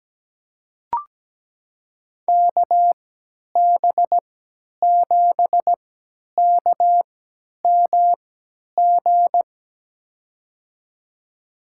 Morse code, heard as a keyed tone, about 17 wpm, spaced as for 10 wpm, 700 Hz. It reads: KB7KMG
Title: USA Call Signs - 6 Characters - 3x Character Spacing - 17wpm